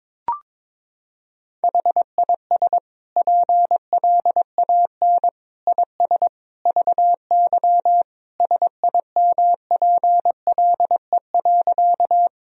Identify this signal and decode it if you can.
Morse code, heard as a keyed tone, about 22 wpm, 700 Hz. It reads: HIS PLAN IS VY SIMPLE.
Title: Sentences from Top 500 Words and Abbreviations 22wpm